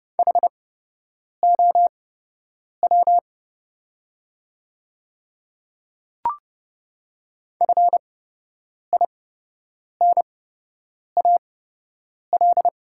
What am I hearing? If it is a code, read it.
Morse code, heard as a keyed tone, about 30 wpm, spaced as for 8 wpm, 690 Hz. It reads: HOW FINAL
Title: 8x Character Spacing - Top 500 Words - 30wpm